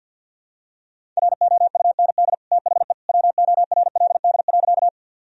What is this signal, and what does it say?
Morse code, heard as a keyed tone, about 47 wpm, 700 Hz. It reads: ROUND THE WORLD.
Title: Sentences from Top 1000 Words - Review Part 1 - 50wpm